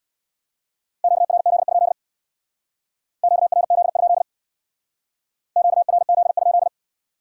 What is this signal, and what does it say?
Morse code, heard as a keyed tone, about 70 wpm, 700 Hz. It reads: QRZ? QRZ? QRZ?